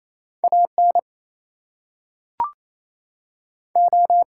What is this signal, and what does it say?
Morse code, heard as a keyed tone, about 27 wpm, 700 Hz. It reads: AN O
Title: Two Letter Words - 28wpm